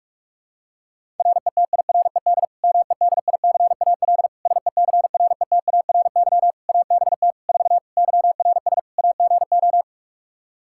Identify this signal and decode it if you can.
Morse code, heard as a keyed tone, about 46 wpm, 700 Hz. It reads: RETIRED MEDICAL SECRETARY ABT 4 YRS AGO
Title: Practice QSO - Exchange 3 - More Info - 45wpm